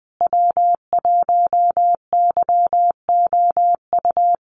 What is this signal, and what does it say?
Morse code, heard as a keyed tone, about 20 wpm, 700 Hz. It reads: W1YOU
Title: USA Call Signs Repeated 3x 20wpm